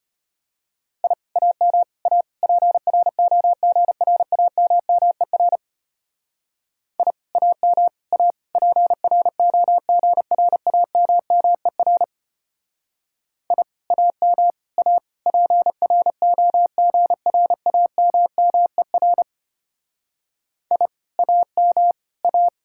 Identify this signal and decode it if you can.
Morse code, heard as a keyed tone, about 31 wpm, 700 Hz. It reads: I AM A PROGRAMMER I AM A PROGRAMMER I AM A PROGRAMMER I AM A
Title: Speed-Racing Common QSO Phrases - 25wpm